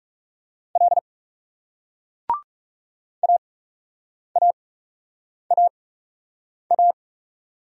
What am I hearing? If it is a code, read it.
Morse code, heard as a keyed tone, about 40 wpm, 700 Hz. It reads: L A A A A